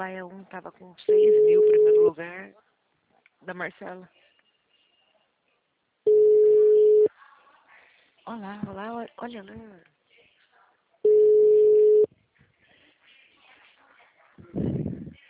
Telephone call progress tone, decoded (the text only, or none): ringback tone